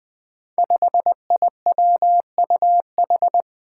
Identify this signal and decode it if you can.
Morse code, heard as a keyed tone, about 20 wpm, 700 Hz. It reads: 5IWUH
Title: Five Challenging Letter-Number Combinations 20wpm